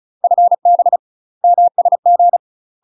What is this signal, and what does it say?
Morse code, heard as a keyed tone, about 36 wpm, 700 Hz. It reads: FB MSG
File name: UK MASTER QSOs TRAINING / 35WPM / UK MASTER QSOs 0011